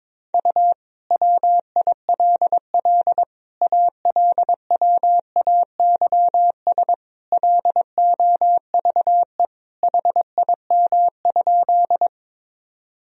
Morse code, heard as a keyed tone, about 22 wpm, 700 Hz. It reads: U WILL ALWAYS LOVE HIM?
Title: Sentences from Top 400 Words and Abbreviations 22wpm